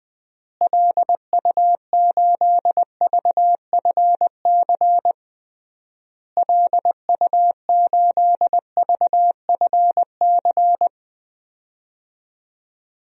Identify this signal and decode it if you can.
Morse code, heard as a keyed tone, about 20 wpm, 700 Hz. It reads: LU8VFC LU8VFC
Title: DXCC-100 Call Signs - 6 Characters - Repeated 3x - 20wpm